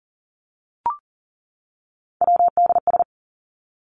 Morse code, heard as a keyed tone, about 40 wpm, 700 Hz. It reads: WDS
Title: Abbreviations 40wpm